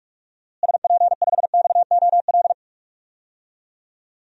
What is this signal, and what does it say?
Morse code, heard as a keyed tone, about 41 wpm, 700 Hz. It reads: SP5XOL
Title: DXCC-100 Call Signs - 6 Characters - Repeated 3x - 45wpm